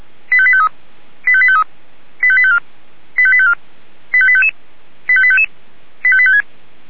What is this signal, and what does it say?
Five-tone selective call (ZVEI), heard as quick runs of five tones, tones of about 70 ms. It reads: 85853 85853 85854 85854 85855 85855 85856